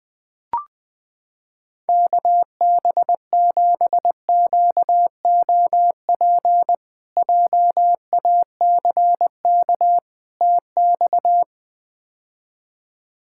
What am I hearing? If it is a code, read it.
Morse code, heard as a keyed tone, about 20 wpm, 700 Hz. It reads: KB7QOP JACK TX
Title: K1USN SST QSO Practice - 20wpm